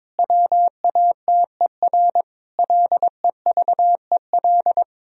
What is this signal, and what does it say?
Morse code, heard as a keyed tone, about 22 wpm, 700 Hz. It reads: WATER LEVEL
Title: Sentences from Top 1000 Words - Review Part 4 - 22wpm